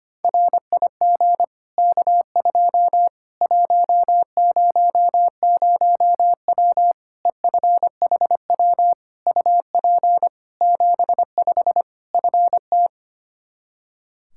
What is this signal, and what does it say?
Morse code, heard as a keyed tone, about 25 wpm, 700 Hz. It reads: RIG K2 100W EFHW UP 75 FT
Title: Practice QSO - Exchange 2 - Info - 25wpm